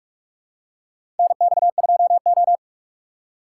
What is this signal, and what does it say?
Morse code, heard as a keyed tone, about 42 wpm, 700 Hz. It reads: NX2O
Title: USA Call Signs - 2x Character Spacing - 45wpm